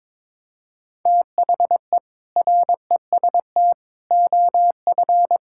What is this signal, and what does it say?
Morse code, heard as a keyed tone, about 22 wpm, 700 Hz. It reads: THE REST OF
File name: practice / 22wpm / Sentences with 4-Word Phrases - 22wpm